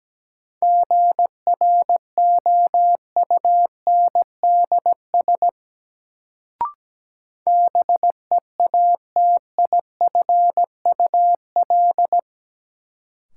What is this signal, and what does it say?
Morse code, heard as a keyed tone, about 17 wpm, 700 Hz. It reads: GROUNDS BEATIFUL
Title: Sentences from Top 1300 Words - New Word Drill - 17wpm